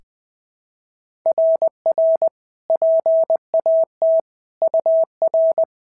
Morse code, heard as a keyed tone, about 20 wpm, 690 Hz. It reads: RR PAT UR
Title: SOTA QSO Practice 20wpm